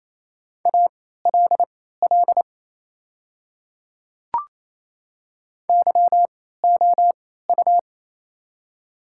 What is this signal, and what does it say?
Morse code, heard as a keyed tone, about 27 wpm, spaced as for 16 wpm, 690 Hz. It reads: ALL YOU
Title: 3x Character Spacing - Top 300 Words - 28wpm